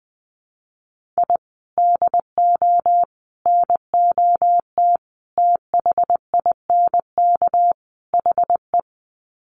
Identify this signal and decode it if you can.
Morse code, heard as a keyed tone, about 20 wpm, 700 Hz. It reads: I DO NOT THINK HE
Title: Sets of 5 Words 20wpm